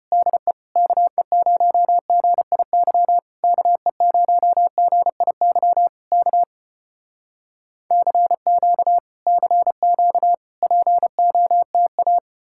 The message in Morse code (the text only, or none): DE KE0GIY KE0GIY K CQ CQ POTA